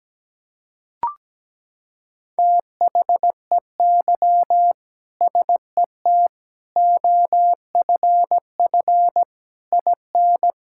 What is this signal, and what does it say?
Morse code, heard as a keyed tone, about 17 wpm, 700 Hz. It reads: THEY SET OFF IN
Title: Sentences from Top 1000 Words - Review Part 5 - 17wpm